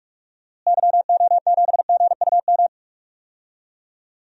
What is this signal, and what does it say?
Morse code, heard as a keyed tone, about 42 wpm, 700 Hz. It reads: YO7GUM
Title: DXCC-100 Call Signs - 6 Characters - Repeated 2x - 45wpm